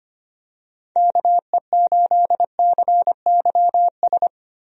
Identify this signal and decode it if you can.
Morse code, heard as a keyed tone, about 25 wpm, 700 Hz. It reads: KE8CYS
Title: USA Call Signs - 6 Characters - Repeated 2x - 25wpm